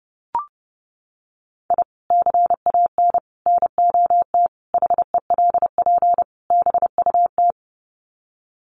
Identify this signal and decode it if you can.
Morse code, heard as a keyed tone, about 30 wpm, 700 Hz. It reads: I CAN NOT HELP BUT
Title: Sets of 5 Words Encore2 30wpm